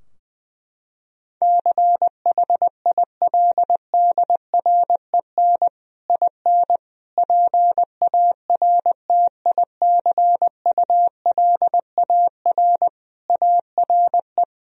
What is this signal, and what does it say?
Morse code, heard as a keyed tone, about 20 wpm, 700 Hz. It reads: CHILDREN IN PARTICULAR ARE
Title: Sentences from Top 1000 Words - Review Part 4 - 20wpm